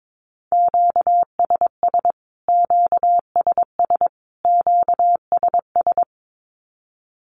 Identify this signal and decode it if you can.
Morse code, heard as a keyed tone, about 22 wpm, 700 Hz. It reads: QSS QSS QSS
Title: Q-Codes Repeated 3x 22wpm